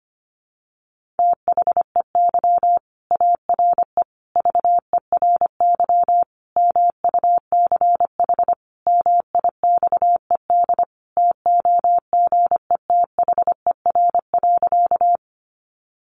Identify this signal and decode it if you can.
Morse code, heard as a keyed tone, about 25 wpm, 700 Hz. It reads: THEY ARE VERY MUCH MIXED TOGETHER.
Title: Sentences from Top 1500 Words 25wpm